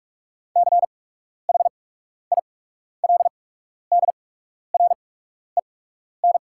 Morse code, heard as a keyed tone, about 44 wpm, spaced as for 12 wpm, 690 Hz. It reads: CHILDREN
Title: 8x Character Spacing - Top 300 Words - 45wpm